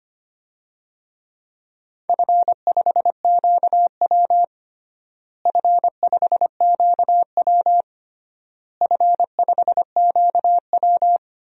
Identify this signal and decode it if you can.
Morse code, heard as a keyed tone, about 25 wpm, 700 Hz. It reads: F5QW F5QW F5QW
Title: DXCC-100 Call Signs - 4 Characters - Repeated 3x - 25wpm